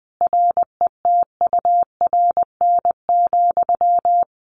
Morse code, heard as a keyed tone, about 20 wpm, 700 Hz. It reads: RETURN,
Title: Sentences from Top 1500 Words 20wpm